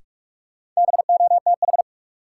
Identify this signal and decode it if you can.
Morse code, heard as a keyed tone, about 43 wpm, 700 Hz. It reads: BOTH
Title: Sentences from Top 300 Words - New Word Drill - 45wpm